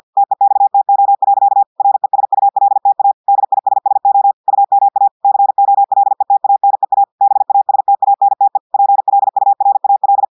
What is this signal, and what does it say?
Morse code, heard as a keyed tone, about 50 wpm, 800 Hz. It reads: TEXTO. RESULTA DISSO UMA COLETANEA BASTANTE PLURAL
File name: audios / portugais / portugais texte en clair 1 vitesse 50 mots minutes